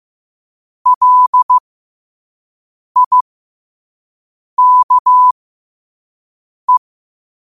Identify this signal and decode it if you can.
Morse code, heard as a keyed tone, about 15 wpm, spaced as for 5 wpm, 1000 Hz. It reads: LIKE
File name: qotdfiles / 5wpm